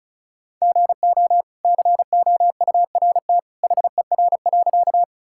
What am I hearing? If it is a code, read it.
Morse code, heard as a keyed tone, about 36 wpm, 700 Hz. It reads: GO COURT HER.
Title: Sentences from Top 1600 Words 35wpm